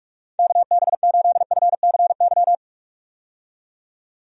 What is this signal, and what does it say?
Morse code, heard as a keyed tone, about 44 wpm, 700 Hz. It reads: KB8FCY
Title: USA Call Signs - 6 Characters - 45wpm